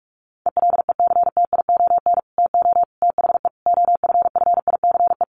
Morse code, heard as a keyed tone, about 44 wpm, 700 Hz. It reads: ELECTION TO THE OFFICE
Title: Sentences from Top 1500 Words 45wpm